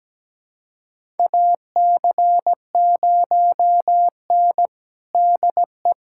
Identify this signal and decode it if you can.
Morse code, heard as a keyed tone, about 17 wpm, 700 Hz. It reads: AC0N DE
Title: USA Callsign DE Callsign 17wpm